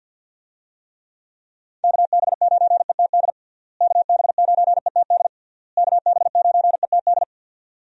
Morse code, heard as a keyed tone, about 51 wpm, 700 Hz. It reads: KB9ETD KB9ETD KB9ETD
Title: USA Call Signs - 6 Characters - Repeated 3x - 50wpm